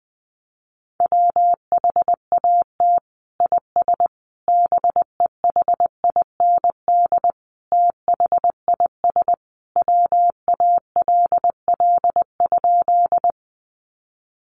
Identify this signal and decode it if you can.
Morse code, heard as a keyed tone, about 20 wpm, 700 Hz. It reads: WHAT IS BEHIND THIS WALL?
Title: Sentences from Top 1100 Words 20wpm